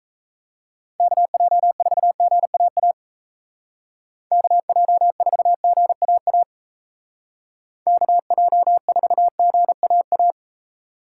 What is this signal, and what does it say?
Morse code, heard as a keyed tone, about 33 wpm, 700 Hz. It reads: KJ4GAA KJ4GAA KJ4GAA
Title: Speed Racing - USA 2x3 Call Signs 28wpm